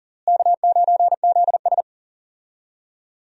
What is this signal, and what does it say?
Morse code, heard as a keyed tone, about 40 wpm, 700 Hz. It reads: K9ZS
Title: USA Call Signs - 4 Characters - Repeated 3x - 40wpm